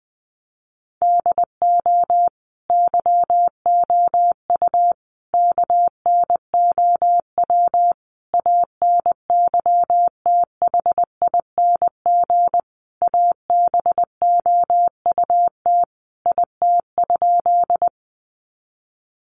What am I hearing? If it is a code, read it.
Morse code, heard as a keyed tone, about 20 wpm, 700 Hz. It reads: DO YOU KNOW ANYTHING ABOUT IT?
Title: Sentences from Top 1000 Words 20wpm